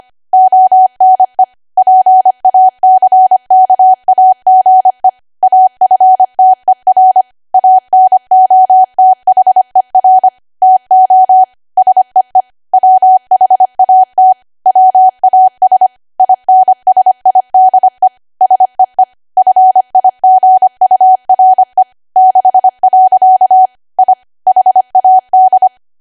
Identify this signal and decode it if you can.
Morse code, heard as a keyed tone, about 25 wpm, 750 Hz. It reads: ONE PACKAGE AFTER ANOTHER TO SEE WHAT WAS INSIDE SEE FIGURE 6. I HAD
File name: cwa / W1AW-Level3 / 25WPM